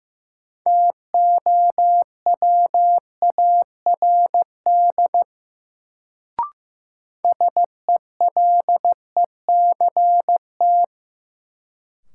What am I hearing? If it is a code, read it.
Morse code, heard as a keyed tone, about 15 wpm, 700 Hz. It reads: TOWARD SELECT
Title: Sentences from Top 1300 Words - New Word Drill Alt - 15wpm